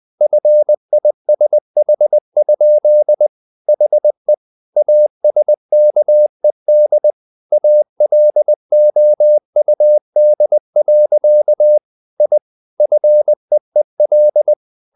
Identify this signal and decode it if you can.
Morse code, heard as a keyed tone, about 20 wpm, 600 Hz. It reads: FISH? HE ASKED ALOUD. I FEEL
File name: the old man and the sea part 2 20wpm